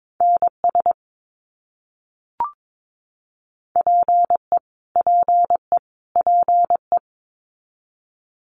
Morse code, heard as a keyed tone, about 22 wpm, 700 Hz. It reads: NS PE PE PE
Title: Canadian Provinces and Territories - Repeated 3x - 22wpm